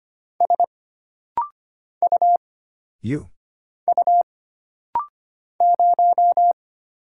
Morse code, heard as a keyed tone, about 25 wpm, 700 Hz. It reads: S U U 0
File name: Single Character (Letter:Number:Punctuation:Prosign) - Rapid-Fire - 25wpm